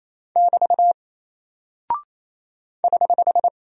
Morse code, heard as a keyed tone, about 28 wpm, 700 Hz. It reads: <BT> <HH>
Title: Single Character (Letter:Number:Punctuation:Prosign) 28wpm